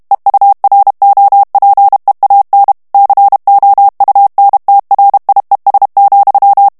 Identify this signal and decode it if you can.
Morse code, heard as a keyed tone, about 32 wpm, 800 Hz. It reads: EUROPEAN COUNTRIES,
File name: ANGLAIS texte clair 2 vitesse 30 mots minute